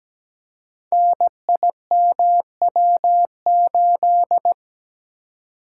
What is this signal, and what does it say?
Morse code, heard as a keyed tone, about 17 wpm, 700 Hz. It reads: NIMW8
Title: Code Groups - 5 Letter:Numbers - 17wpm